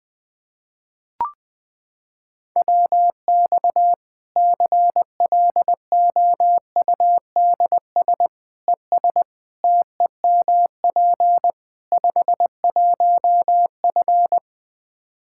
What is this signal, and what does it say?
Morse code, heard as a keyed tone, about 20 wpm, 700 Hz. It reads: WX CLOUDS ES TEMP 51F
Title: Practice QSO - Exchange 2 - Info - 20wpm